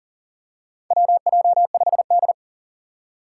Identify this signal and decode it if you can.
Morse code, heard as a keyed tone, about 40 wpm, 700 Hz. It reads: WJ5D
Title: USA Call Signs - 4 Characters - 3x Character Spacing - 40wpm